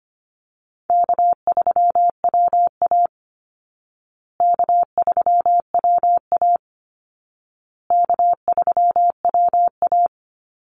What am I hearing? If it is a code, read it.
Morse code, heard as a keyed tone, about 25 wpm, 700 Hz. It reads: K3WA K3WA K3WA
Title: Call Signs Repeated 2x - 25wpm